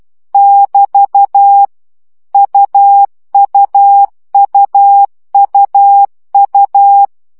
Morse code, heard as a keyed tone, about 12 wpm, 800 Hz. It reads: <BT> UUUUU
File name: CW gyakorlas u 10 baud